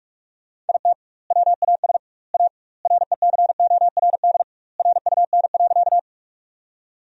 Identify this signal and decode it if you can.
Morse code, heard as a keyed tone, about 41 wpm, 700 Hz. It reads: IT WAS A RECORD RUN.
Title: 2x Word-Spacing - Sentences from Top 500 Words - 45wpm